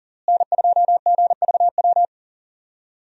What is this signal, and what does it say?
Morse code, heard as a keyed tone, about 40 wpm, 700 Hz. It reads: N2GVW
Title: USA Call Signs - 5 Characters - Repeated 3x - 40wpm